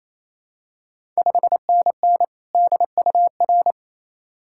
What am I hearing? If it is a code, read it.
Morse code, heard as a keyed tone, about 28 wpm, 700 Hz. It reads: 5NN DUR
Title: Ontario QSO Party Exchange 28wpm